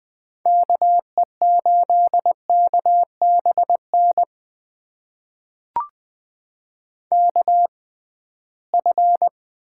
Morse code, heard as a keyed tone, about 21 wpm, spaced as for 15 wpm, 700 Hz. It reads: KE8KBN KF
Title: USA Call Signs - 6 Characters - 6x Character Spacing - 20wpm